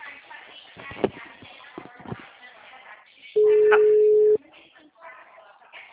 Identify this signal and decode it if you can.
Ringback tone (Europe, 425 Hz), with speech.
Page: Pirassununga